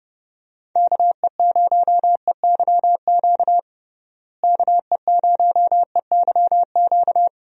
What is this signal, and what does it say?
Morse code, heard as a keyed tone, about 30 wpm, 700 Hz. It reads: KE0EYQ KE0EYQ
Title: USA Call Signs - 6 Characters - Repeated 2x - 30wpm